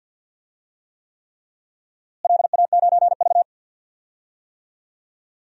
Morse code, heard as a keyed tone, about 51 wpm, 700 Hz. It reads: LA9V
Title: DXCC-100 Call Signs - 4 Characters - Repeated 3x - 50wpm